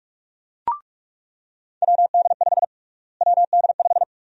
Morse code, heard as a keyed tone, about 45 wpm, 700 Hz. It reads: WD5 WD5
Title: USA Call Sign Prefixes - Repeated 2x - 45wpm